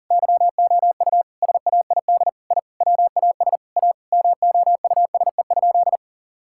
Morse code, heard as a keyed tone, about 37 wpm, 700 Hz. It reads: YOU SAID I WAS A MOUSE?
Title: Sentences from Top 1000 Words - Review Part 5 - 40wpm